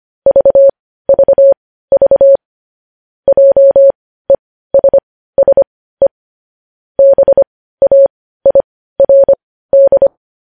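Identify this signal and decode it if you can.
Morse code, heard as a keyed tone, about 25 wpm, spaced as for 15 wpm, 550 Hz. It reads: VVV JESSE BAIRD
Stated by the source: News headlines sent in Morse code at 15 wpm.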